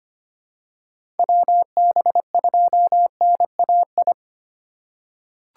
Morse code, heard as a keyed tone, about 25 wpm, 700 Hz. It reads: WB2NAI
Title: USA Call Signs - 6 Characters - 2x Character Spacing - 25wpm